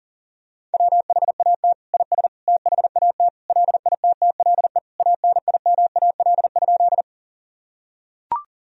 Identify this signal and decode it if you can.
Morse code, heard as a keyed tone, about 40 wpm, 700 Hz. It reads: WHAT IS THAT LITTLE ANIMAL?
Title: Sentences from Top 1000 Words - Review Part 1 - 40wpm